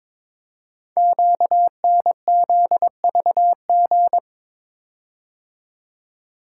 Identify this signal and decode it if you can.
Morse code, heard as a keyed tone, about 22 wpm, 690 Hz. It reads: QNZVG
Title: Code Groups - 5 Letters - 22wpm